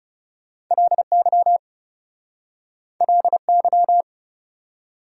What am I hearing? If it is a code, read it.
Morse code, heard as a keyed tone, about 38 wpm, 700 Hz. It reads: LY LY
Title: Speed-Racing Prefixes and Suffixes 30wpm